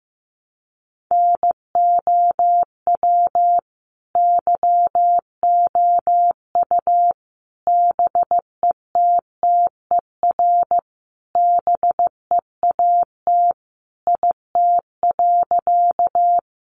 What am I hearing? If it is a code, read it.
Morse code, heard as a keyed tone, about 15 wpm, 700 Hz. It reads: NOW YOU BETTER BEAT IT.
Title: Sentences from Top 1300 Words 15wpm